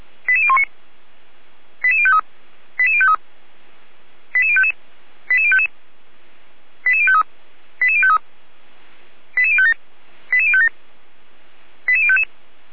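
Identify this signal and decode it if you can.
Five-tone selective call (ZVEI), heard as quick runs of five tones, tones of about 70 ms. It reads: 88019 88053 88053 88055 88055 88053 88053 88057 88057 88055
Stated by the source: ZVEI five-tone call received by SDR, logged on server of feuerwehr-orlamuende.org (fire brigade Orlamünde).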